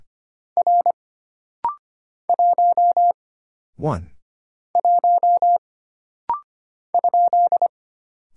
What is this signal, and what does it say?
Morse code, heard as a keyed tone, about 25 wpm, 700 Hz. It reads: R 1 1 ?
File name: Single Character (Letter:Number:Punctuation:Prosign) - Rapid-Fire - 25wpm